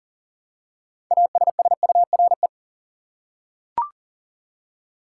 Morse code, heard as a keyed tone, about 40 wpm, 700 Hz. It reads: ASSURE
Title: Six Letter Words - 40wpm